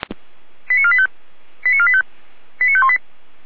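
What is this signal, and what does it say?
Five-tone selective call (ZVEI), heard as quick runs of five tones, tones of about 70 ms. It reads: 89485 89485 89518
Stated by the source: ZVEI five-tone call received by SDR, recorded by feuerwehr-orlamuende.org fire brigade.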